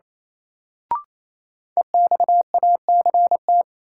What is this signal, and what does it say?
Morse code, heard as a keyed tone, about 28 wpm, 700 Hz. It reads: EXACT
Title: Sentences from Top 1300 Words - New Word Drill Alt - 28wpm